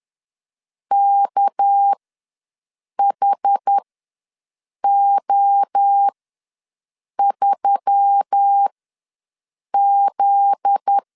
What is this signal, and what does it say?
Morse code, heard as a keyed tone, about 10 wpm, spaced as for 6 wpm, 800 Hz. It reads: KHO3Z
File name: LEZIONI / 7 GF9Z / 7 GF9Z ws 10wpm 3919